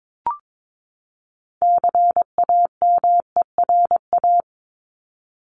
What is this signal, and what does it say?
Morse code, heard as a keyed tone, about 22 wpm, 700 Hz. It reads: CAMERA
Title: Top 501-750 Words 22wpm